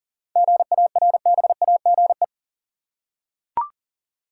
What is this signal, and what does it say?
Morse code, heard as a keyed tone, about 40 wpm, 700 Hz. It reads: GARBAGE
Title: Seven Letter Words - 40wpm